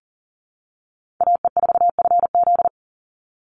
Morse code, heard as a keyed tone, about 37 wpm, 700 Hz. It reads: AE4FZ
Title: NA SOTA Chaser Callsigns 40wpm